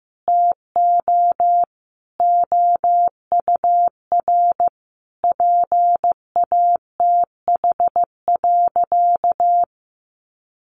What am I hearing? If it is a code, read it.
Morse code, heard as a keyed tone, about 15 wpm, 700 Hz. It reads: TO OUR PATH.
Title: Sentences from Top 1400 Words 15wpm